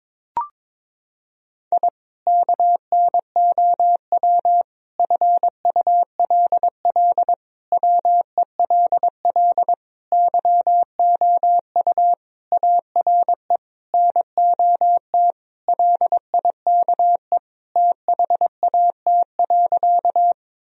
Morse code, heard as a keyed tone, about 22 wpm, 700 Hz. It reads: I KNOW FULL WELL YOU ARE NOT LIKE THAT.
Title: Sentences from Top 1000 Words - Review Part 3 - 22wpm